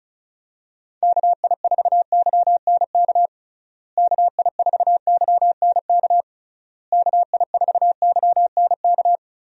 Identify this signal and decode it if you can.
Morse code, heard as a keyed tone, about 34 wpm, 700 Hz. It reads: KI4YNK KI4YNK KI4YNK
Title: USA Call Signs - 6 Characters - Repeated 3x - 35wpm